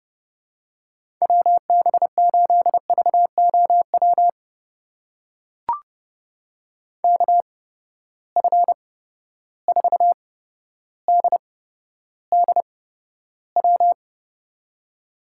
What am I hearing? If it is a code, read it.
Morse code, heard as a keyed tone, about 32 wpm, spaced as for 13 wpm, 700 Hz. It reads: WB8VOW KF4DDW
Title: USA Call Signs - 6 Characters - 8x Character Spacing - 30wpm